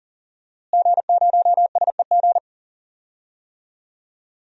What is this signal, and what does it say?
Morse code, heard as a keyed tone, about 37 wpm, 700 Hz. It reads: G0SEG
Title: DXCC-100 Call Signs - 5 Characters - Repeated 3x - 40wpm